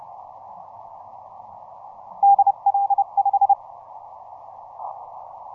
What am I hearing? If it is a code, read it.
Morse code, heard as a keyed tone, about 29 wpm, 800 Hz. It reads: DL5
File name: simple-file-list / KH6MB 15mcw